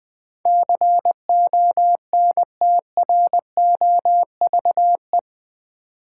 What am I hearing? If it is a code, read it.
Morse code, heard as a keyed tone, about 20 wpm, 700 Hz. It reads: CONTROVE
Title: Build A Word - 4 Letters At A Time - 11-Letter Words - 20wpm